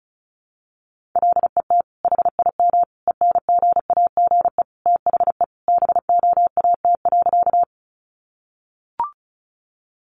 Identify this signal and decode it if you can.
Morse code, heard as a keyed tone, about 34 wpm, 700 Hz. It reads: LET HIM ENGAGE THE BOAT.
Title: Sentences from Top 1500 Words 35wpm